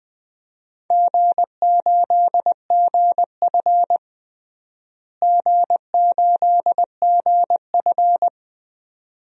Morse code, heard as a keyed tone, about 21 wpm, 700 Hz. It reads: G8GF G8GF
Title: DXCC-100 Call Signs - 4 Characters - Repeated 2x - 20wpm